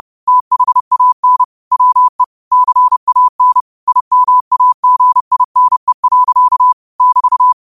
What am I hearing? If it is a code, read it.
Morse code, heard as a keyed tone, about 30 wpm, 1000 Hz. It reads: THAN WE CAN IMAGINE. <BT>